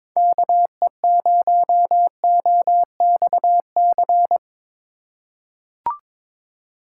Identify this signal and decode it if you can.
Morse code, heard as a keyed tone, about 22 wpm, 700 Hz. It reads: KE0OXC
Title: USA Call Signs - 6 Characters - 8x Character Spacing - 22wpm